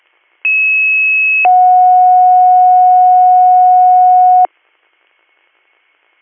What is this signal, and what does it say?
Two-tone page, heard as a one-second tone, then a three-second tone, about 2600 then 730 Hz. It is heard from 0.4 to 4.5 s.